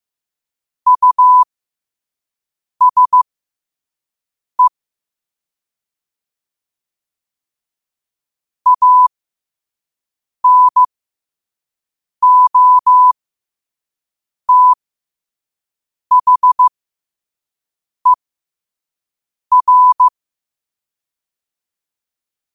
Morse code, heard as a keyed tone, about 15 wpm, spaced as for 5 wpm, 1000 Hz. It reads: USE ANOTHER
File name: qotdfiles / 5wpm